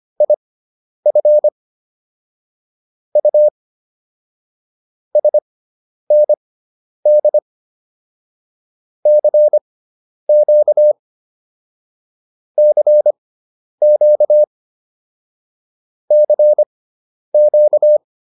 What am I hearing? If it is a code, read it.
Morse code, heard as a keyed tone, about 25 wpm, spaced as for 10 wpm, 600 Hz. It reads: IF U SND CQ CQ CQ